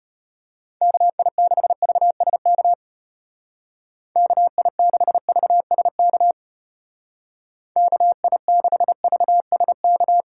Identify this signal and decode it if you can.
Morse code, heard as a keyed tone, about 32 wpm, 700 Hz. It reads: KI6VSK KI6VSK KI6VSK